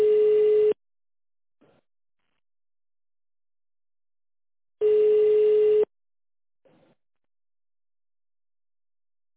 Ringback tone (Europe, 425 Hz).